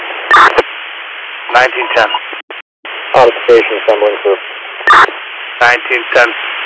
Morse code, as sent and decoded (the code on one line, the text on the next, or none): none